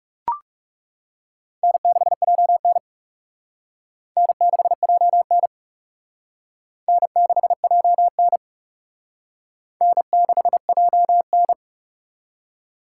Morse code, heard as a keyed tone, about 36 wpm, 700 Hz. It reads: N6JN N6JN N6JN N6JN
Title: Speed Racing - USA Call Signs 30wpm